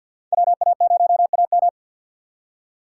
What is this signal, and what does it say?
Morse code, heard as a keyed tone, about 52 wpm, 700 Hz. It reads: WA0AM
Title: USA Call Signs - 5 Characters - 6x Character Spacing - 50wpm